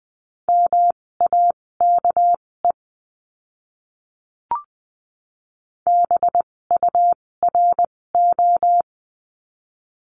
Morse code, heard as a keyed tone, about 20 wpm, spaced as for 16 wpm, 700 Hz. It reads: MAKE BURO
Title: 256 Lesson 42a - Words 12-20wpm